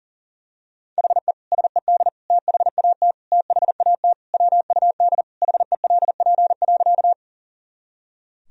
Morse code, heard as a keyed tone, about 40 wpm, 700 Hz. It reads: HE SED THAT THAT WUD HELP.
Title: Sentences from Top 200 Words and Abbreviations 40wpm